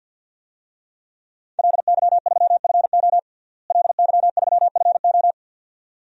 Morse code, heard as a keyed tone, about 52 wpm, 700 Hz. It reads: LY3FO LY3FO
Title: DXCC-100 Call Signs - 5 Characters - Repeated 2x - 50wpm